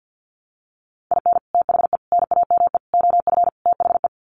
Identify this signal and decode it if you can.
Morse code, heard as a keyed tone, about 50 wpm, 700 Hz. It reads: IN THE NAME OF THE
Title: Sets of 5 Words Encore 50wpm